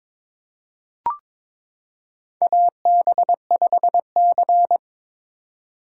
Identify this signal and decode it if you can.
Morse code, heard as a keyed tone, about 22 wpm, 700 Hz. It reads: AB5C
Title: K1USN SST QSO Practice - 22wpm